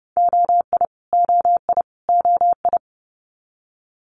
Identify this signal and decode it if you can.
Morse code, heard as a keyed tone, about 30 wpm, 700 Hz. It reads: OI OI OI
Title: Multi-letter Phonemes Repeated 3x 30wpm